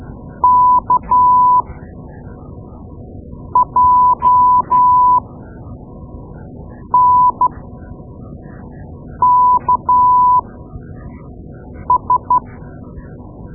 Morse code, heard as a keyed tone, about 12 wpm, spaced as for 4 wpm, 1000 Hz. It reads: KJNKS
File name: Morse Lessons / Morse Lessons / Tape-2A